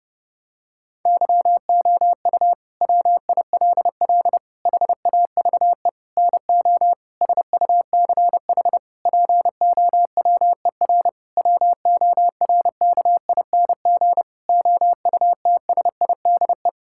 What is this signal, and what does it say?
Morse code, heard as a keyed tone, about 30 wpm, 700 Hz. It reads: YOU WILL HAVE NO SUCH POWER WORKING OUTSIDE
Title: Sentences from Top 1000 Words - Review Part 6 - 30wpm